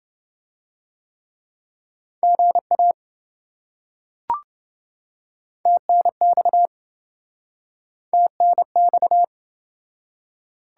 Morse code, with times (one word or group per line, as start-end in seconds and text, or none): 2.2-2.9 GA
5.7-6.7 TNX
8.1-9.2 TNX